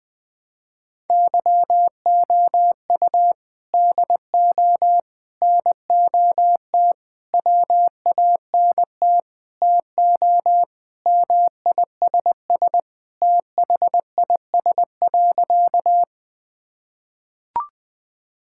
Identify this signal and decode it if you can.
Morse code, heard as a keyed tone, about 20 wpm, 700 Hz. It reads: YOU DO NOT WANT TO MISS THIS.
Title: Sentences with 4-Word Phrases - 20wpm